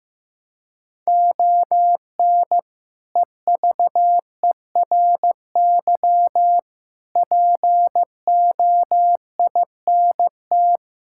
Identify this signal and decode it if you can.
Morse code, heard as a keyed tone, about 15 wpm, 700 Hz. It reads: ON EVERY POINT
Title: Sentences from Top 1000 Words - Review Part 6 - 15wpm